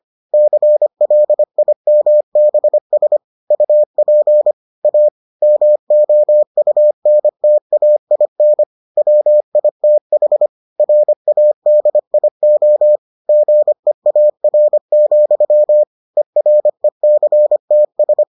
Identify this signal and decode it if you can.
Morse code, heard as a keyed tone, about 25 wpm, 600 Hz. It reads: CLIMBS UP A MOUNTAIN WITH RADIO GEAR, ERECTS